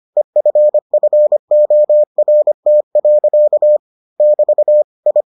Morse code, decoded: EFFORT. <BT> I

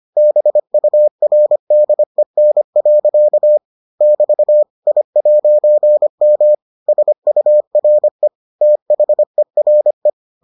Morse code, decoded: BURDEN. <BT> I'M SURE THERE

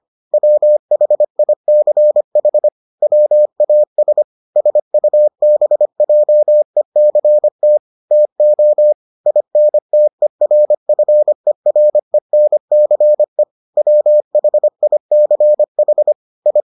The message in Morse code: WHICH WAS SUBJECT TO INTERFERENCE WHICH I

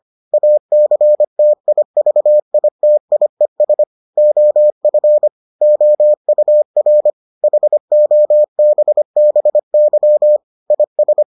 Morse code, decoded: ACTIVITIES OF OUR HOBBY IS

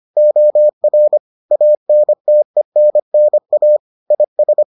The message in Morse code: OR ANTENNA IS

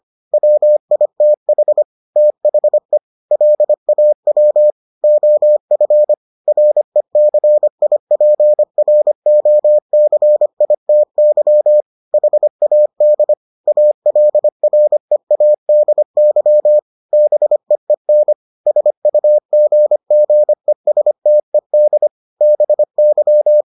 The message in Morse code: WITH THE LAW OF RECIPROCITY HAD ALREADY BEEN SUGGESTED BY

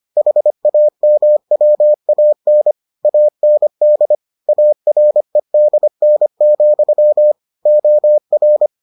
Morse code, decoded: HAMWAN AND AREDN, OR

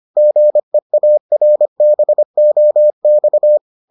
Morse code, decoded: GEARBOX

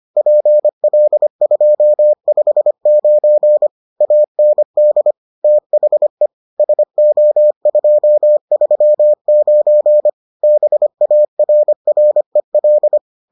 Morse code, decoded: PL259 AND THE SO239 BARREL

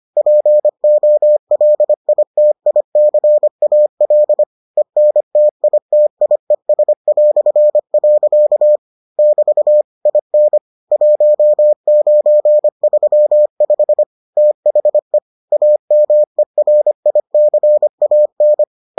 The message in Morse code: POLITICAL ENTITIES". <BT> IN 1935 THE AMERICAN

